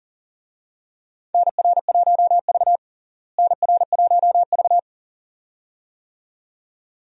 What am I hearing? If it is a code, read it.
Morse code, heard as a keyed tone, about 40 wpm, 700 Hz. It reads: NR1V NR1V
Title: USA Call Signs - 4 Characters - Repeated 2x - 40wpm